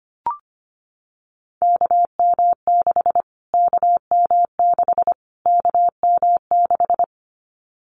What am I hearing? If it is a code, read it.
Morse code, heard as a keyed tone, about 25 wpm, 700 Hz. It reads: KM6 KM6 KM6
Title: Callsign Prefixes for Top 25 DXCC - Repeated 3x - 25wpm